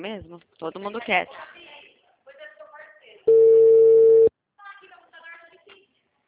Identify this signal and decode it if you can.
Ringback tone (China, 450 Hz), with speech.